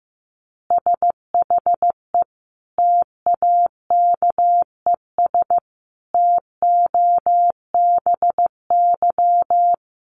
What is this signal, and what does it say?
Morse code, heard as a keyed tone, about 14 wpm, 700 Hz. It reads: SHE TAKES TOBY
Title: Easy Phrases 15wpm